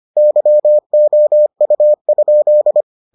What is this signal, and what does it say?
Morse code, heard as a keyed tone, about 25 wpm, 600 Hz. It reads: YOU?